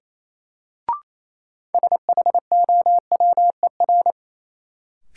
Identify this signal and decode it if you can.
Morse code, heard as a keyed tone, about 28 wpm, 700 Hz. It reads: SHOWER
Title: Sentences from Top 1600 Words - New Word Drill - 28wpm